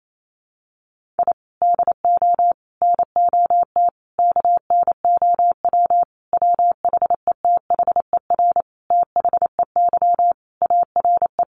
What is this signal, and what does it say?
Morse code, heard as a keyed tone, about 28 wpm, 700 Hz. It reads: I DO NOT KNOW WHETHER THEY ARE
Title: Sentences from Top 1500 Words 28wpm